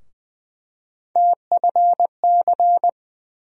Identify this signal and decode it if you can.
Morse code, heard as a keyed tone, about 20 wpm, 700 Hz. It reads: TFC
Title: USA Call Sign Postfixes - 3 Letters 20wpm